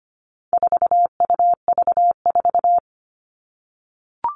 Morse code, heard as a keyed tone, about 25 wpm, 700 Hz. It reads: 4UV4
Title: All Cut Numbers - 4-Digit 25wpm